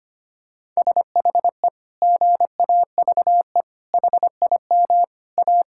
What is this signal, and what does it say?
Morse code, heard as a keyed tone, about 25 wpm, 700 Hz. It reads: SHE GAVE HIM A
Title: Sentences from Top 1600 Words 25wpm